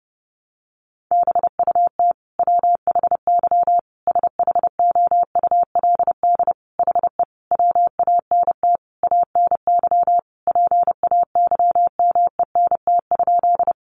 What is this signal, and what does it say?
Morse code, heard as a keyed tone, about 30 wpm, 700 Hz. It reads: BUT WHY SHOULD HE WANT ANY PAYMENT?
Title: Sentences from Top 1200 Words 30wpm